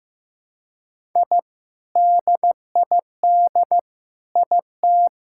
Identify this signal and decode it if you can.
Morse code, heard as a keyed tone, about 15 wpm, 700 Hz. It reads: I DID IT
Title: Sentences from Top 1000 Words - Review Part 6 - 15wpm